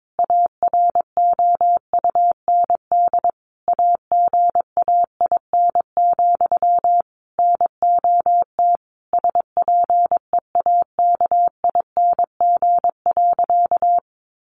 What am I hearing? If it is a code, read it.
Morse code, heard as a keyed tone, about 22 wpm, 700 Hz. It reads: AROUND AGAIN, NOT SPEAKING.
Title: A Sound Of Thunder 22wpm